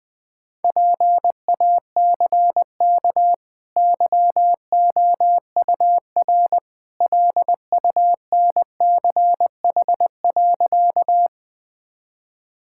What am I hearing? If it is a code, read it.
Morse code, heard as a keyed tone, about 21 wpm, 700 Hz. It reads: PACK YOUR LUNCH.
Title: Easy Sentences with Hints - 20wpm